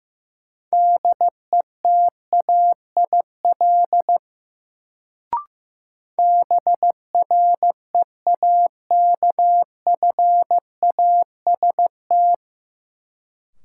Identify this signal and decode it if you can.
Morse code, heard as a keyed tone, about 15 wpm, 700 Hz. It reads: DETAIL BREAKFAST
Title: Sentences from Top 1500 Words - New Word Drill - 15wpm